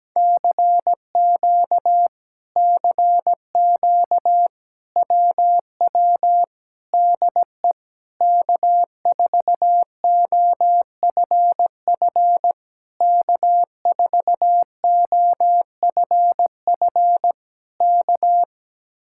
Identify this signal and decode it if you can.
Morse code, heard as a keyed tone, about 17 wpm, 700 Hz. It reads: CQ CQ WW DE K4OFF K4OFF K
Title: Speed-Racing Common QSO Phrases - 17wpm